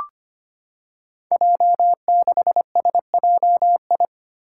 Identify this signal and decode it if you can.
Morse code, heard as a keyed tone, about 25 wpm, 700 Hz. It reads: J6SJI
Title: Five Challenging Letter-Number Combinations 25wpm